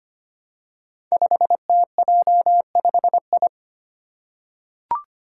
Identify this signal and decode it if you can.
Morse code, heard as a keyed tone, about 25 wpm, 700 Hz. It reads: 5TJ5I
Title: Five Challenging Letter-Number Combinations 25wpm